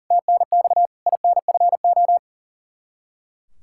Morse code, heard as a keyed tone, about 40 wpm, 700 Hz. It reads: TNX INFO
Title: Practice QSO - Exchange 3 - More Info - 40wpm